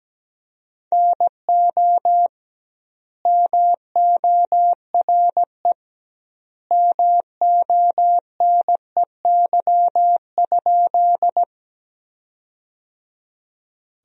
Morse code, heard as a keyed tone, about 17 wpm, 700 Hz. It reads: NO MORE MONEY?
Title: 2x Word-Spacing - Sentences from Top 400 Words - 17wpm